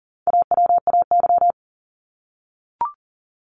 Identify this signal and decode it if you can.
Morse code, heard as a keyed tone, about 40 wpm, 700 Hz. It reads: AWAY
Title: Top 501-750 Words 40wpm